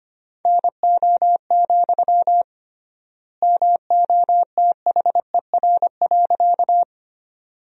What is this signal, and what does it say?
Morse code, heard as a keyed tone, about 25 wpm, 700 Hz. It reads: NO, MOTHER.
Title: 3x Word-Spacing - Sentences from Top 200 Words - 25wpm